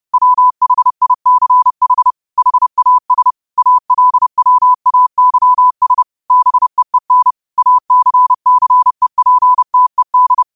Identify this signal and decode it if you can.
Morse code, heard as a keyed tone, about 30 wpm, 1000 Hz. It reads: WHICH HAS ALWAYS BEEN ACCEPTED
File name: qotdfiles / 30wpm